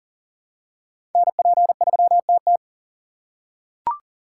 Morse code, heard as a keyed tone, about 40 wpm, 700 Hz. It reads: NP3TT
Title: USA Call Signs - 8x Character Spacing - 40wpm